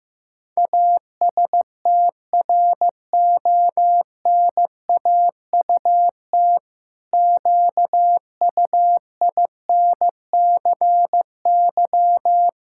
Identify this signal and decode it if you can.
Morse code, heard as a keyed tone, about 16 wpm, 700 Hz. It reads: ASTRONAUT QUINCY
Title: Pangram Challenge v2 - 15wpm